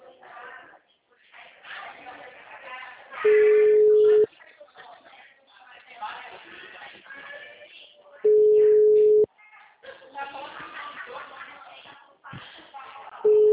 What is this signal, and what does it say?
Ringback tone (Europe, 425 Hz), with speech.
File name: out- -2014- - - .76855